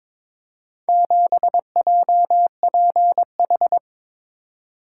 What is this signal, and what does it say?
Morse code, heard as a keyed tone, about 23 wpm, 690 Hz. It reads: 7JPH
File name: practice / 22wpm / Four Letter-Number Combinations - 22wpm